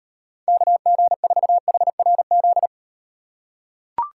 Morse code, heard as a keyed tone, about 38 wpm, 700 Hz. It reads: KG4HRZ